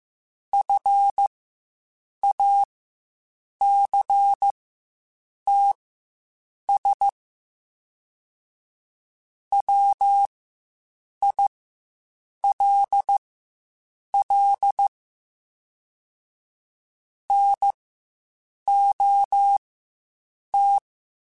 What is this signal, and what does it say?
Morse code, heard as a keyed tone, about 15 wpm, spaced as for 7 wpm, 800 Hz. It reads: FACTS WILL NOT
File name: Quotes Spaced / Quotes Spaced-mp3s / IgnoringFacts sp @10